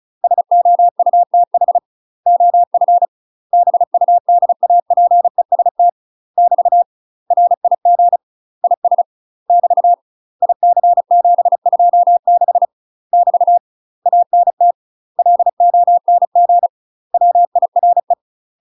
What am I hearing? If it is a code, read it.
Morse code, heard as a keyed tone, about 34 wpm, 700 Hz. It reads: SOUTH OF BUDAPEST <BT> RIG IS <BT> IC726 <BT> ANT LONG WIRE
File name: UK MASTER QSOs TRAINING / 35WPM / UK MASTER QSOs 0005